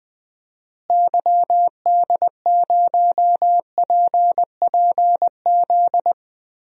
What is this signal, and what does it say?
Morse code, heard as a keyed tone, about 21 wpm, 700 Hz. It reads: YD0PPZ
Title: DXCC-100 Call Signs - 6 Characters - Repeated 3x - 20wpm